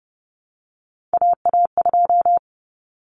Morse code, heard as a keyed tone, about 30 wpm, 700 Hz. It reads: AA2
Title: Common Cut Numbers - 3-Digit 30wpm